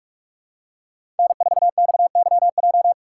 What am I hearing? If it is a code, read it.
Morse code, heard as a keyed tone, about 43 wpm, 700 Hz. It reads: N4XYJ
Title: USA Call Signs - 5 Characters - 2x Character Spacing - 45wpm